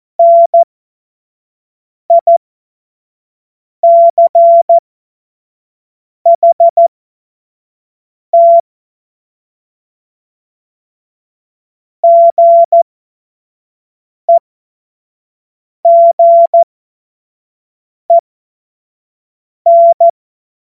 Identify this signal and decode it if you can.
Morse code, heard as a keyed tone, about 14 wpm, spaced as for 5 wpm, 680 Hz. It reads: NICHT GEGEN
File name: Morsekurs / Witze / Witz 03. August 2021, 25 F70